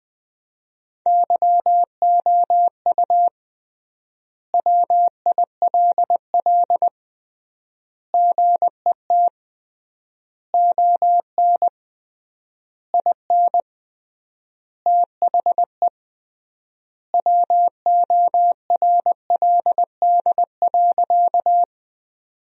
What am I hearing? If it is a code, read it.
Morse code, heard as a keyed tone, about 20 wpm, 700 Hz. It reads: YOU WILL GET ON IN THE WORLD.
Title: 3x Word-Spacing - Sentences from Top 200 Words - 20wpm